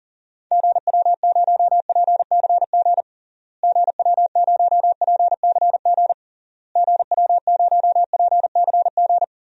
Morse code, heard as a keyed tone, about 40 wpm, 700 Hz. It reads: GW0PCG GW0PCG GW0PCG
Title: DXCC-100 Call Signs - 6 Characters - Repeated 3x - 40wpm